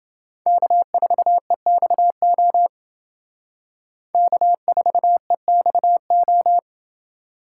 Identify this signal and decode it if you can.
Morse code, heard as a keyed tone, about 29 wpm, 700 Hz. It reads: K4EXO K4EXO